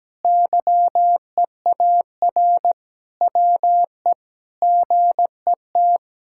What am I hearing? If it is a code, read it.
Morse code, heard as a keyed tone, about 17 wpm, 700 Hz. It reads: YEAR WE GET
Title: Sentences from Top 1600 Words 17wpm